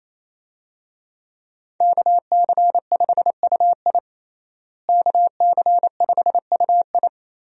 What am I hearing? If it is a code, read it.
Morse code, heard as a keyed tone, about 28 wpm, 700 Hz. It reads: KC5UI KC5UI
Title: USA Call Signs - 5 Characters - Repeated 3x - 28wpm